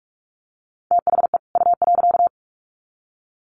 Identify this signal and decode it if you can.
Morse code, heard as a keyed tone, about 44 wpm, 700 Hz. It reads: THE U.
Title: Sets of 2 Words 45wpm